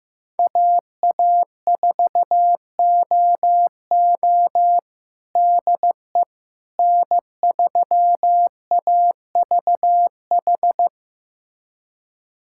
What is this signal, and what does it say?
Morse code, heard as a keyed tone, about 15 wpm, 700 Hz. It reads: AA4OO DE N3AVH
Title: USA Callsign DE Callsign - 5 Characters - 15wpm